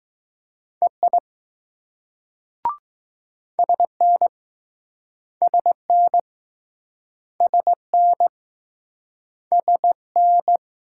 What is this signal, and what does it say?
Morse code, heard as a keyed tone, about 20 wpm, 700 Hz. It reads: EI SN SN SN SN